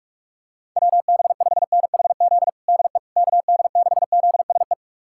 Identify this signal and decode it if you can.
Morse code, heard as a keyed tone, about 41 wpm, 700 Hz. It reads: WB5NHZ DE KD6ZSE
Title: USA Callsign DE Callsign 45wpm